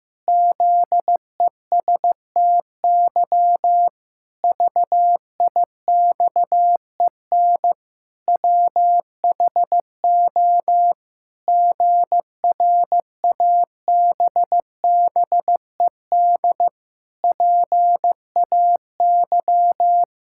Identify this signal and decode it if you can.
Morse code, heard as a keyed tone, about 15 wpm, 700 Hz. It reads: ZESTY VIXEN WHO GRABBED PAY
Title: Pangram Challenge v2 - 15wpm